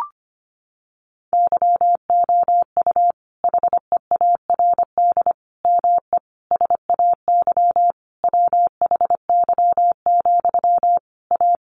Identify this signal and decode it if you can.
Morse code, heard as a keyed tone, about 25 wpm, 700 Hz. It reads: YOU HEARD ME SAY WHY, A
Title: Sentences from Top 500 Words 25wpm